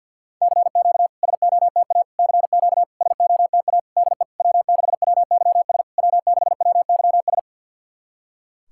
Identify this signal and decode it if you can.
Morse code, heard as a keyed tone, about 50 wpm, 700 Hz. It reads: CQ SOTA CQ SOTA DE W6WYS W6WYS
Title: SOTA QSO Practice 50wpm